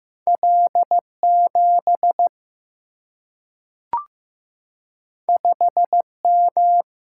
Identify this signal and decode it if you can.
Morse code, heard as a keyed tone, about 15 wpm, 700 Hz. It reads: L 7 5 M
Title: Single-Letter Single-Number 15wpm